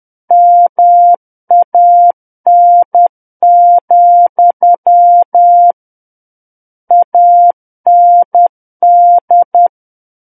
Morse code, heard as a keyed tone, about 10 wpm, 700 Hz. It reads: MAN, AND
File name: Book One / 01-Chapter 1 - 10 WPM